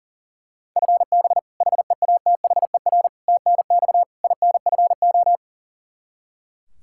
Morse code, heard as a keyed tone, about 37 wpm, 700 Hz. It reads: FB HEATHER TNX INFO
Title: Practice QSO - Exchange 3 - More Info - 40wpm